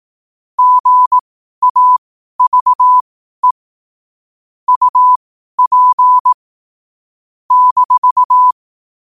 Morse code, heard as a keyed tone, about 18 wpm, spaced as for 12 wpm, 1000 Hz. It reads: GAVE UP -
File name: qotdfiles / 12wpm